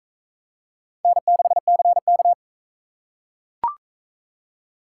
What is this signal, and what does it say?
Morse code, heard as a keyed tone, about 42 wpm, 700 Hz. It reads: N6CK